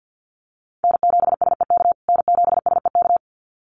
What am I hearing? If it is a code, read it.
Morse code, heard as a keyed tone, about 50 wpm, 700 Hz. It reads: N7SEK N7SEK
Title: USA Call Signs Repeated 2x 50wpm